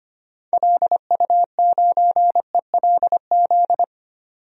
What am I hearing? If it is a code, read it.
Morse code, heard as a keyed tone, about 25 wpm, 700 Hz. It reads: LU9ELZ
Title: DXCC-100 Call Signs - 6 Characters - 25wpm